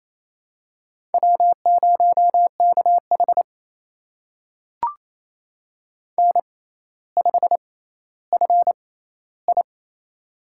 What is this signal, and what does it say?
Morse code, heard as a keyed tone, about 27 wpm, spaced as for 14 wpm, 700 Hz. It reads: W0KH N5FI
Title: USA Call Signs - 4 Characters - 6x Character Spacing - 28wpm